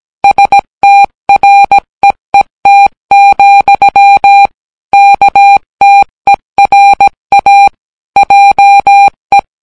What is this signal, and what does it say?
Morse code, heard as a keyed tone, about 17 wpm, 800 Hz. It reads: STREET, KTERA JE